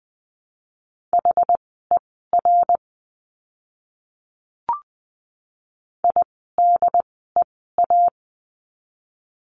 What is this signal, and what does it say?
Morse code, heard as a keyed tone, about 20 wpm, spaced as for 14 wpm, 700 Hz. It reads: HER IDEA
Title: 108 Lesson 18f - Letter m - Words 20wpm